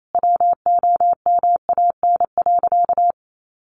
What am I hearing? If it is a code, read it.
Morse code, heard as a keyed tone, about 28 wpm, 700 Hz. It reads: WOMAN.
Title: Sentences from Top 1500 Words 28wpm